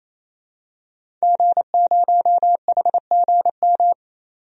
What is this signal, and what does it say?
Morse code, heard as a keyed tone, about 28 wpm, 700 Hz. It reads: G0HGM
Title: DXCC-100 Call Signs - 5 Characters - 28wpm